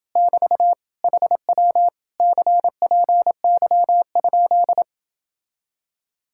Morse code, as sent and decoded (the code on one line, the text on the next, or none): -...- / .... .-- / -.-. .--. -.-- ..--..
<BT> HW CPY?